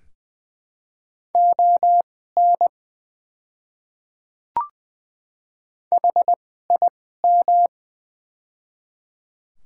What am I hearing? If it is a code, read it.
Morse code, heard as a keyed tone, about 20 wpm, spaced as for 14 wpm, 700 Hz. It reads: ON HIM
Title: 2x Character Spacing - Top 100 Words - 20wpm